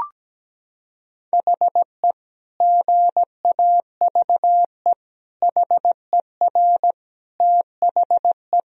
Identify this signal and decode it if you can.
Morse code, heard as a keyed tone, about 17 wpm, 700 Hz. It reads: HE GAVE HER THE
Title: Sentences from Top 1000 Words - Review Part 1 - 17wpm